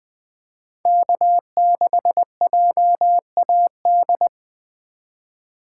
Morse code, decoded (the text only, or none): K6JAD